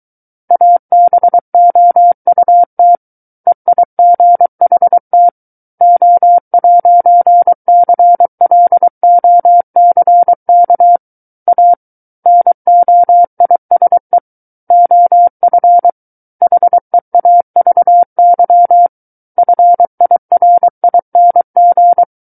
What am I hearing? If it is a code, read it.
Morse code, heard as a keyed tone, about 23 wpm, 700 Hz. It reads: ABOUT EIGHT O'CLOCK A NOISE OF HEAVY FIRING
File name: Book One / 14-Chapter 14 - 23 WPM